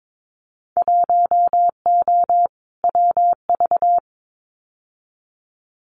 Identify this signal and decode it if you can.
Morse code, heard as a keyed tone, about 22 wpm, 700 Hz. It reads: 1O WV
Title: Winter Field Day Exchanges 22wpm